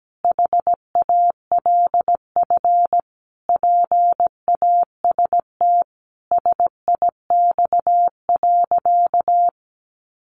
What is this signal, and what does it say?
Morse code, heard as a keyed tone, about 17 wpm, 700 Hz. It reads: HALF PAST SIX.
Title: Sentences from Top 600 Words 17wpm